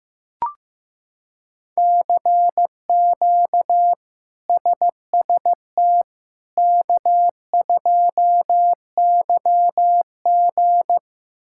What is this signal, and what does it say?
Morse code, heard as a keyed tone, about 15 wpm, 700 Hz. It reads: CQ SST K2YG
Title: K1USN SST QSO Practice - 15wpm